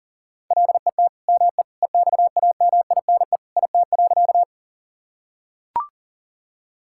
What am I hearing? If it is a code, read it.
Morse code, heard as a keyed tone, about 40 wpm, 700 Hz. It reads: LET ME EXAMINE IT.
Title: Sentences from Top 1600 Words 40wpm